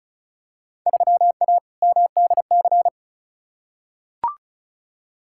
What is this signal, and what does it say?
Morse code, heard as a keyed tone, about 34 wpm, 700 Hz. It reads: 3A MDC
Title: Field Day Exchanges - 35wpm